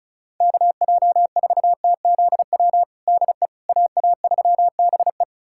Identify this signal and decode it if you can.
Morse code, heard as a keyed tone, about 34 wpm, 700 Hz. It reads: KJ4TZW DE AA3BE
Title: USA Callsign DE Callsign 35wpm